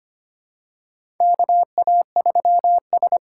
Morse code, heard as a keyed tone, about 25 wpm, 700 Hz. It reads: KA3S
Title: USA Call Signs - 8x Character Spacing - 25wpm